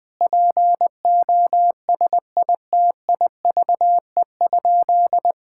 Morse code, heard as a keyed tone, about 20 wpm, 700 Hz. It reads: POSITIVE?
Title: Sentences from Top 1000 Words - Review Part 5 - 20wpm